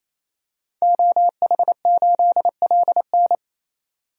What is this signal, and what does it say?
Morse code, heard as a keyed tone, about 28 wpm, 700 Hz. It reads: OH8LN
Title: DXCC-100 Call Signs - 5 Characters - 28wpm